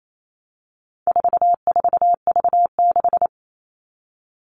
Morse code, heard as a keyed tone, about 28 wpm, 700 Hz. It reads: 44V6
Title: All Cut Numbers - 4-Digit 28wpm